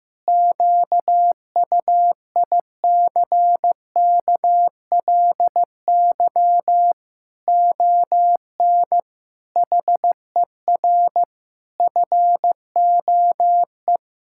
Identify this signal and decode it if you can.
Morse code, heard as a keyed tone, about 15 wpm, 700 Hz. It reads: QUICKLY ON HER FOE
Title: Pangram Challenge v2 - 15wpm